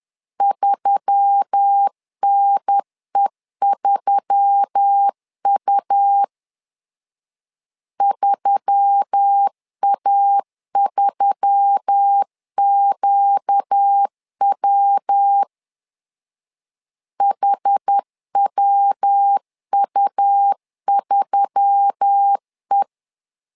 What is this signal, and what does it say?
Morse code, heard as a keyed tone, about 10 wpm, 800 Hz. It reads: 3NE3U 3A3QW HWU3E
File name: LEZIONI / 4 Q1S3 10wpm 2427